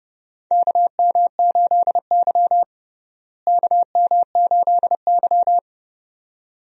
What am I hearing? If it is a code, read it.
Morse code, heard as a keyed tone, about 32 wpm, 700 Hz. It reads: KM8Y KM8Y
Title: USA Call Signs - 4 Characters - Repeated 2x - 30wpm